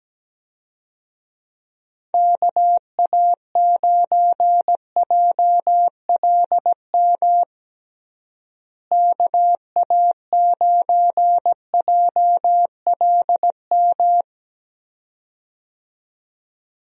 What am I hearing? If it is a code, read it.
Morse code, heard as a keyed tone, about 17 wpm, 700 Hz. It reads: KA9JLM KA9JLM
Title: USA Call Signs - 6 Characters - Repeated 2x - 17wpm